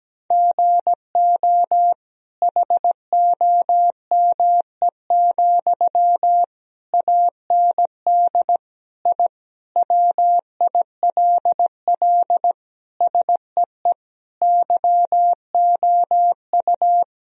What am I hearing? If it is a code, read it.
Morse code, heard as a keyed tone, about 17 wpm, 700 Hz. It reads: GO HOME, AND I WILL SEE YOU
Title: Sentences from Top 1000 Words - Review Part 4 - 17wpm